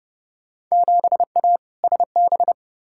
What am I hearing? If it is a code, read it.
Morse code, heard as a keyed tone, about 30 wpm, 700 Hz. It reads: 7A SB
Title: Field Day Exchanges Encore - 30wpm